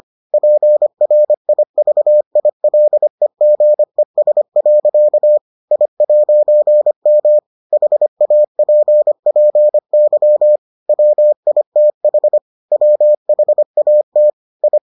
Morse code, heard as a keyed tone, about 25 wpm, 600 Hz. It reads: PRIVILEGES. I'M HAPPY WITH WHAT I